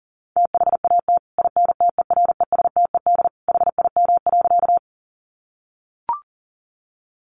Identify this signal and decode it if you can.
Morse code, heard as a keyed tone, about 40 wpm, 700 Hz. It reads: THAT INTERESTED HIM.
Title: Sentences from Top 900 Words 40wpm